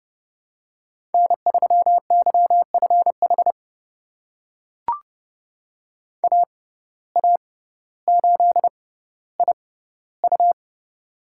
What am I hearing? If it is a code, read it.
Morse code, heard as a keyed tone, about 30 wpm, spaced as for 15 wpm, 700 Hz. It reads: N3YFH AA8IU
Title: USA Call Signs - 5 Characters - 6x Character Spacing - 30wpm